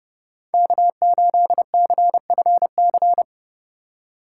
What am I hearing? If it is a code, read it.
Morse code, heard as a keyed tone, about 32 wpm, 700 Hz. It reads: K8CFC
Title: USA Call Signs - 5 Characters - 2x Character Spacing - 30wpm